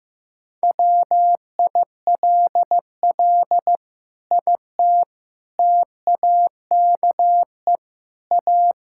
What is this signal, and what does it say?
Morse code, heard as a keyed tone, about 15 wpm, 700 Hz. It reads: WILL IT TAKE A
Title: Sentences from Top 1000 Words - Review Part 6 - 15wpm